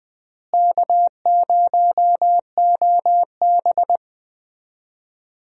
Morse code, heard as a keyed tone, about 20 wpm, 700 Hz. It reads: K0OB
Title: USA Call Signs - 4 Characters - Repeated 3x - 20wpm